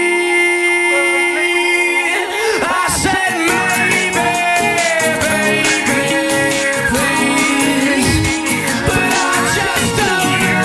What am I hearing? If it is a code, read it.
Dial-up modem handshake.